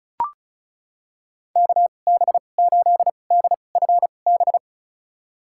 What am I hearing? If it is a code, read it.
Morse code, heard as a keyed tone, about 34 wpm, spaced as for 25 wpm, 700 Hz. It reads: KB8DFB
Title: USA Call Signs - 6 Characters - 2x Character Spacing - 35wpm